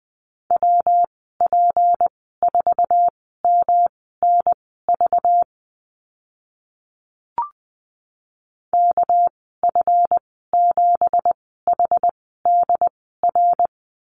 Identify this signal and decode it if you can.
Morse code, heard as a keyed tone, about 20 wpm, spaced as for 15 wpm, 700 Hz. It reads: WP4MNV KF7HDR
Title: 200 Lesson 32g - Letter k - Call Signs 20wpm